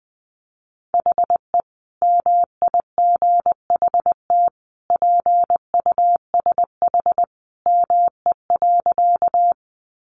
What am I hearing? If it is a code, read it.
Morse code, heard as a keyed tone, about 20 wpm, 700 Hz. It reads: HE MIGHT PUSH ME.
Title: Sentences from Top 1300 Words 20wpm